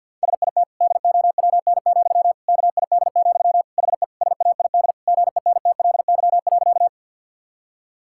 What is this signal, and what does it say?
Morse code, heard as a keyed tone, about 50 wpm, 700 Hz. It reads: SIT DOWN, KID, HE SAID GENTLY.
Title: Sentences from Top 1600 Words 50wpm